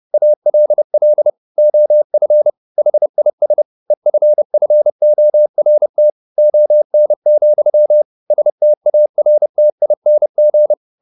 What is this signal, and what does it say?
Morse code, heard as a keyed tone, about 30 wpm, 600 Hz. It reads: ALL OF HIS EFFORT ON, STARTING